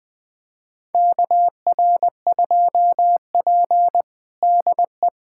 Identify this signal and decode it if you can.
Morse code, heard as a keyed tone, about 20 wpm, 700 Hz. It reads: KR2P DE
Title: USA Callsign DE Callsign - 4 Characters - 20wpm